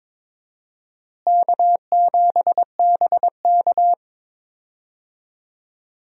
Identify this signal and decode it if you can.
Morse code, heard as a keyed tone, about 22 wpm, 700 Hz. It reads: K7BK
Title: USA Call Signs - 4 Characters - Repeated 3x - 22wpm